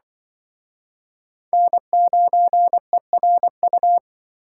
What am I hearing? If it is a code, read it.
Morse code, heard as a keyed tone, about 26 wpm, 700 Hz. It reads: N9ERU